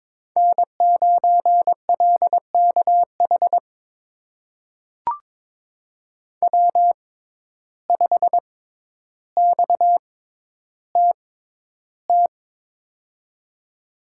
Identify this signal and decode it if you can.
Morse code, heard as a keyed tone, about 21 wpm, spaced as for 11 wpm, 700 Hz. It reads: N9LKH W5XTT
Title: USA Call Signs - 5 Characters - 6x Character Spacing - 22wpm